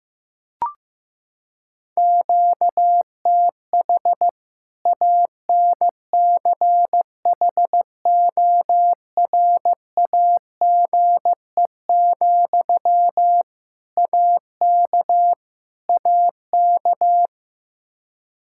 Morse code, heard as a keyed tone, about 16 wpm, 700 Hz. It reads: QTH ANCHORAGE, AK AK
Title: Practice QSO - Exchange 1 - Beginning - 15wpm